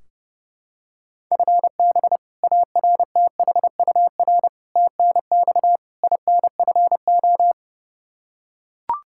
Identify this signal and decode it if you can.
Morse code, heard as a keyed tone, about 32 wpm, 700 Hz. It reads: FB ARTHUR TNX INFO
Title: Practice QSO - Exchange 3 - More Info - 30wpm